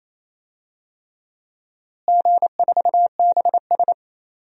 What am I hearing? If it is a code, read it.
Morse code, heard as a keyed tone, about 28 wpm, 700 Hz. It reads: G4BS
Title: DXCC-100 Call Signs - 4 Characters - Repeated 3x - 28wpm